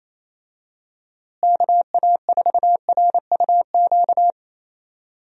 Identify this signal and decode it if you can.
Morse code, heard as a keyed tone, about 28 wpm, 700 Hz. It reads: KA4RUQ
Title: USA Call Signs - 6 Characters - Repeated 2x - 28wpm